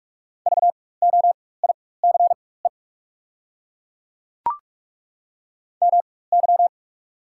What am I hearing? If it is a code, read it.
Morse code, heard as a keyed tone, about 43 wpm, spaced as for 21 wpm, 700 Hz. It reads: VOICE MY
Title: 4x Character Spacing - Top 500 Words - 45wpm